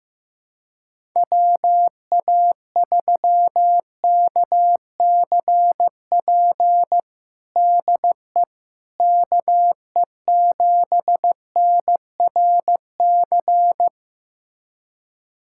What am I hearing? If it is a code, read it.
Morse code, heard as a keyed tone, about 15 wpm, 700 Hz. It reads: WA3KCP DE KE7NRC
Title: USA Callsign DE Callsign - 6 Characters - 15wpm